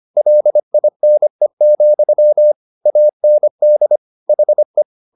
Morse code, heard as a keyed tone, about 25 wpm, 600 Hz. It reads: LINE, AND HE
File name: the old man and the sea part 1 25wpm